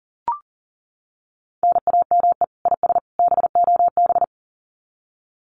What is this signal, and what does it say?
Morse code, heard as a keyed tone, about 40 wpm, 700 Hz. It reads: NAME IS BOB
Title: Common QSO Phrases 40wpm